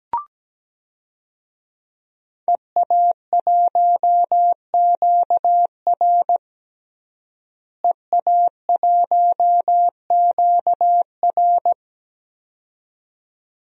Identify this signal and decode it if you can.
Morse code, heard as a keyed tone, about 17 wpm, 700 Hz. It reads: EA1QR EA1QR
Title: DXCC-100 Call Signs - 5 Characters - Repeated 2x - 17wpm